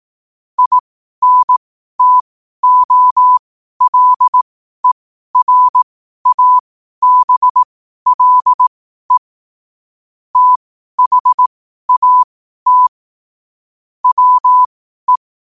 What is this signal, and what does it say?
Morse code, heard as a keyed tone, about 18 wpm, spaced as for 12 wpm, 1000 Hz. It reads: INTOLERABLE THAT WE
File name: qotdfiles / 12wpm